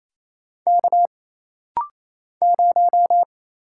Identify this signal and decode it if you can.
Morse code, heard as a keyed tone, about 28 wpm, 700 Hz. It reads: K 0
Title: Single Letter-Number - Rapid-Fire 28wpm